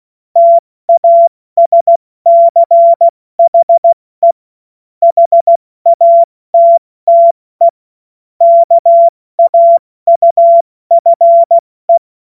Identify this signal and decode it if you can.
Morse code, heard as a keyed tone, about 16 wpm, 680 Hz. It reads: TASCHE HATTE KAUFE
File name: Morsekurs / Witze / Witz 22. Februar 2022 70 F80